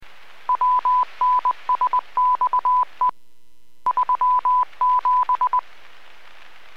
Morse code, heard as a keyed tone, about 20 wpm, 1000 Hz. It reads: WNSXE 37